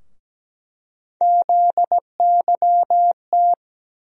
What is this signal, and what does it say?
Morse code, heard as a keyed tone, about 17 wpm, 700 Hz. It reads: ZYT
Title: USA Call Sign Postfixes - 3 Letters 17wpm